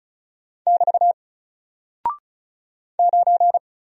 Morse code, heard as a keyed tone, about 35 wpm, 700 Hz. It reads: <BT> 9
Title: Single Character (Letter:Number:Punctuation:Prosign) 35wpm